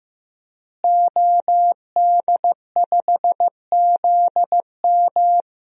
Morse code, heard as a keyed tone, about 15 wpm, 700 Hz. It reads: OD5ZM
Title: DXCC-100 Call Signs - 5 Characters - Repeated 3x - 15wpm